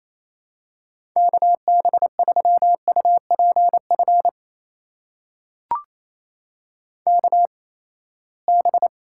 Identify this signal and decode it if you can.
Morse code, heard as a keyed tone, about 28 wpm, spaced as for 19 wpm, 700 Hz. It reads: KB3UPF KB
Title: USA Call Signs - 6 Characters - 8x Character Spacing - 28wpm